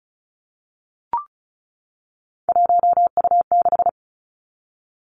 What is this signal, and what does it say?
Morse code, heard as a keyed tone, about 36 wpm, 700 Hz. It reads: 1U6
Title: Three Challenging Letter-Number Combinations 35wpm